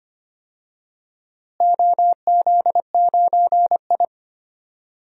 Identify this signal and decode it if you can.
Morse code, heard as a keyed tone, about 25 wpm, 700 Hz. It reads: OZ9I
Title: DXCC-100 Call Signs - 4 Characters - Repeated 3x - 25wpm